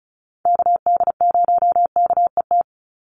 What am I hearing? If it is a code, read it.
Morse code, heard as a keyed tone, about 34 wpm, 700 Hz. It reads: KD0KET
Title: USA Call Signs 35wpm